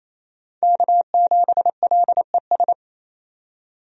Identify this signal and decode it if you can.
Morse code, heard as a keyed tone, about 28 wpm, 700 Hz. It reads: K7LES
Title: USA Call Signs - 5 Characters - 28wpm